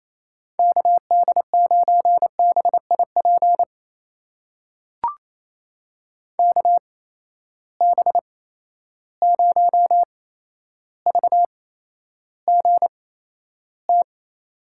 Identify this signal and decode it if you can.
Morse code, heard as a keyed tone, about 28 wpm, spaced as for 12 wpm, 700 Hz. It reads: KD9BIP KB0VGT
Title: USA Call Signs - 6 Characters - 8x Character Spacing - 28wpm